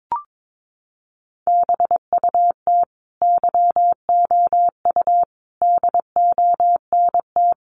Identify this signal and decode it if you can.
Morse code, heard as a keyed tone, about 22 wpm, 700 Hz. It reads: BUT YOU DONT
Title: Sentences from Top 600 Words 22wpm